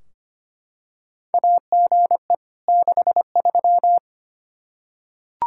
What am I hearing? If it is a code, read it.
Morse code, heard as a keyed tone, about 25 wpm, 700 Hz. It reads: AGE 63
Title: K1USN SST QSO Practice 25wpm